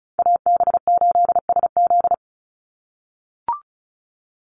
Morse code, heard as a keyed tone, about 34 wpm, 700 Hz. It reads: AB8SZ
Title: USA Call Signs 35wpm